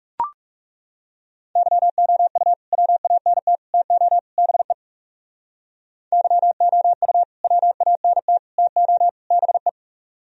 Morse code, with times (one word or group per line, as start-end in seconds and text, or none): 1.6-2.5 YOU
2.7-3.6 WANT
3.7-4.2 TO
4.4-4.7 BE
6.1-7.2 YOU
7.4-8.4 WANT
8.6-9.1 TO
9.3-9.7 BE